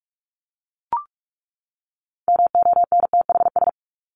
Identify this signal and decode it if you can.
Morse code, heard as a keyed tone, about 44 wpm, 700 Hz. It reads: MONTHS
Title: ES:S Suffix - Top 100 Words with Base Words 45wpm